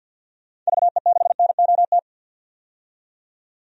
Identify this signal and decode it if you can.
Morse code, heard as a keyed tone, about 46 wpm, 700 Hz. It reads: VE6NOT
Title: DXCC-100 Call Signs - 6 Characters - 50wpm